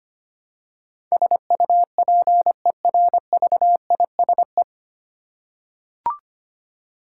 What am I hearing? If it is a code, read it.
Morse code, heard as a keyed tone, about 25 wpm, 700 Hz. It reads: SUPERVISE
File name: practice / 25wpm / 9 Letter Words - First 3 Letters Given - 25wpm